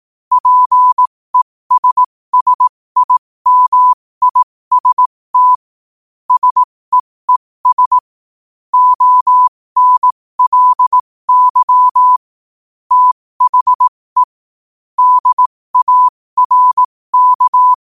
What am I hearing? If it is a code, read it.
Morse code, heard as a keyed tone, about 18 wpm, spaced as for 15 wpm, 1000 Hz. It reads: PESSIMIST SEES ONLY THE DARK
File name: qotdfiles / 15wpm